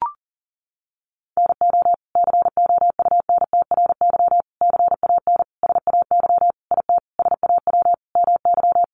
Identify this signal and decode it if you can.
Morse code, heard as a keyed tone, about 40 wpm, 690 Hz. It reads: NO COUNTRY CAN SAY IT SAW MY
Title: Sentences from Top 800 Words 40wpm